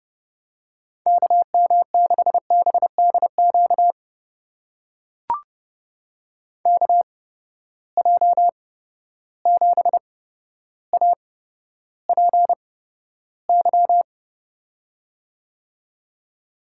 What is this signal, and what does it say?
Morse code, heard as a keyed tone, about 30 wpm, spaced as for 13 wpm, 700 Hz. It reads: KM6BDQ KJ7APY
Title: USA Call Signs - 6 Characters - 8x Character Spacing - 30wpm